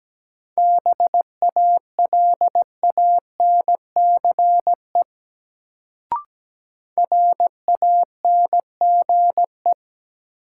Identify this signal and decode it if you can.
Morse code, heard as a keyed tone, about 17 wpm, 700 Hz. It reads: BALANCE RANGE
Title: Sentences from Top 900 Words - New Word Drill - 17wpm